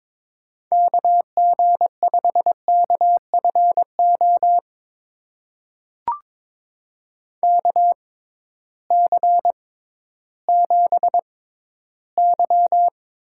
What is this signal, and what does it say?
Morse code, heard as a keyed tone, about 21 wpm, spaced as for 13 wpm, 700 Hz. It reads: KG5KFO KC7Y
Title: USA Call Signs - 6 Characters - 6x Character Spacing - 22wpm